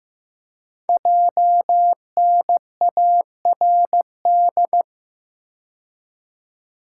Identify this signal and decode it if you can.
Morse code, heard as a keyed tone, about 15 wpm, 700 Hz. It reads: JNARD
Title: Code Groups - 5 Letters - 15wpm